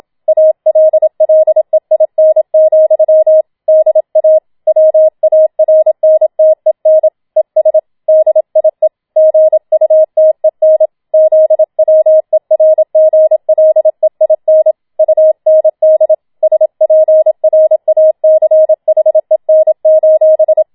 Morse code, heard as a keyed tone, about 27 wpm, 630 Hz. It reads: ALLEIN, DA WARNTEN ES DIE GUTEN ZWERGLEIN UND SPRACHEN:
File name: Schneewittchen Teil2 27wpm